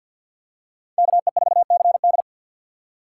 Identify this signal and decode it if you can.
Morse code, heard as a keyed tone, about 51 wpm, 700 Hz. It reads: KE4CD
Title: USA Call Signs - 5 Characters - 4x Character Spacing - 50wpm